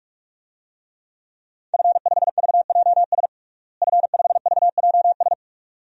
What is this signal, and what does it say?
Morse code, heard as a keyed tone, about 46 wpm, 700 Hz. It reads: F5VJS F5VJS
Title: DXCC-100 Call Signs - 5 Characters - Repeated 2x - 45wpm